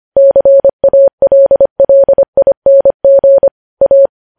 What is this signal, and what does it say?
Morse code, heard as a keyed tone, about 25 wpm, 550 Hz. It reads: CALLING A
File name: podcast / News headlines-1x-25x25